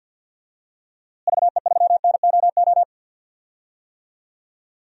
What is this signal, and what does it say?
Morse code, heard as a keyed tone, about 53 wpm, 700 Hz. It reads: VE3NOO
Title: DXCC-100 Call Signs - 6 Characters - Repeated 2x - 50wpm